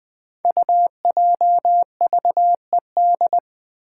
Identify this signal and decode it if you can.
Morse code, heard as a keyed tone, about 20 wpm, 700 Hz. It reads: UJVED
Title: Five Challenging Letter-Number Combinations 20wpm